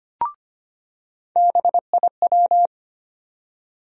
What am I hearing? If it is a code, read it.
Morse code, heard as a keyed tone, about 25 wpm, 700 Hz. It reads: BIW
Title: USA Call Sign Postfixes - 3 Letters 25wpm